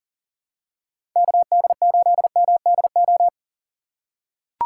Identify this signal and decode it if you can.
Morse code, heard as a keyed tone, about 40 wpm, 700 Hz. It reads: KD8MDO
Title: USA Call Signs - 6 Characters - 6x Character Spacing - 40wpm